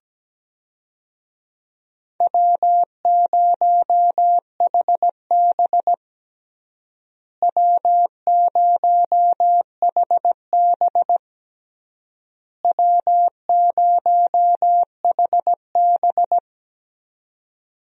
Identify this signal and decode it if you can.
Morse code, heard as a keyed tone, about 17 wpm, 700 Hz. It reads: W0HB W0HB W0HB
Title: USA Call Signs - 4 Characters - Repeated 3x - 17wpm